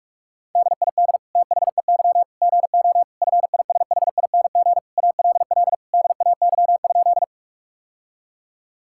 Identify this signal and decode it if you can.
Morse code, heard as a keyed tone, about 48 wpm, 700 Hz. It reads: DID THEY GO FISHING ALL DAY?
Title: Sentences from Top 1000 Words - Review Part 5 - 45wpm